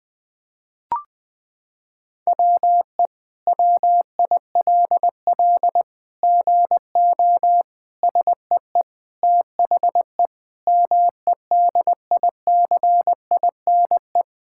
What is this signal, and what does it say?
Morse code, heard as a keyed tone, about 20 wpm, 700 Hz. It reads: WE WILL GO SEE THE MEDICINE
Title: Sentences from Top 1000 Words - Review Part 6 - 20wpm